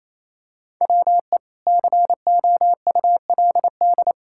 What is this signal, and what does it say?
Morse code, heard as a keyed tone, about 28 wpm, 700 Hz. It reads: WE COULD
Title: Pangram Challenge v2 - 28wpm